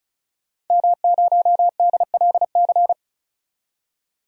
Morse code, heard as a keyed tone, about 34 wpm, 700 Hz. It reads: M0DLC
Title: DXCC-100 Call Signs - 5 Characters - 35wpm